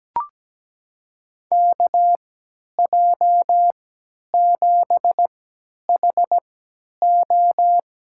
Morse code, heard as a keyed tone, about 17 wpm, spaced as for 10 wpm, 700 Hz. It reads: KJ7HO
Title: USA Call Signs - 3x Character Spacing - 17wpm